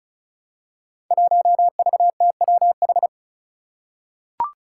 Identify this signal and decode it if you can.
Morse code, heard as a keyed tone, about 34 wpm, 700 Hz. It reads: 1VTWH
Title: Five Challenging Letter-Number Combinations 35wpm